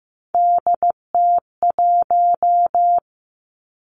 Morse code, heard as a keyed tone, about 15 wpm, 700 Hz. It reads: DT1
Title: Three Challenging Letter-Number Combinations 15wpm